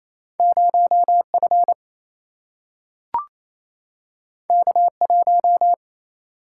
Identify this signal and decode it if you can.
Morse code, heard as a keyed tone, about 28 wpm, 700 Hz. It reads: 0 F K 1
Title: Single-Letter Single-Number 28wpm